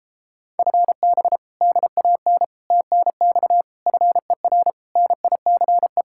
Morse code, heard as a keyed tone, about 33 wpm, 700 Hz. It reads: FB DAN TNX FER NICE